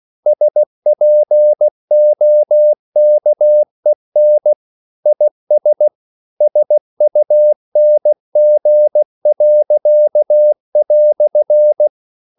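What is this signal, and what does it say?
Morse code, heard as a keyed tone, about 16 wpm, 600 Hz. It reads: SPOKEN IS SUNG."
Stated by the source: English quotations sent in CW as practice at 16 words per minute.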